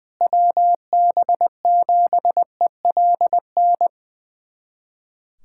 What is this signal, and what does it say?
Morse code, heard as a keyed tone, about 20 wpm, 700 Hz. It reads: WB7ELN
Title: USA Callsign DE Callsign 20wpm